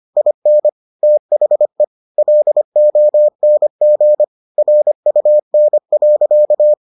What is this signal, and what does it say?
Morse code, heard as a keyed tone, about 25 wpm, 600 Hz. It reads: IN THE LONG RUN.